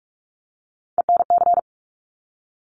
Morse code, heard as a keyed tone, about 46 wpm, 700 Hz. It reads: ENC
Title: Top Three Letter Combinations in Words 45wpm